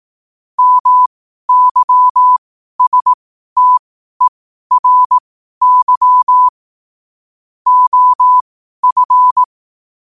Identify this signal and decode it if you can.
Morse code, heard as a keyed tone, about 18 wpm, spaced as for 12 wpm, 1000 Hz. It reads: MYSTERY OF
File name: qotdfiles / 12wpm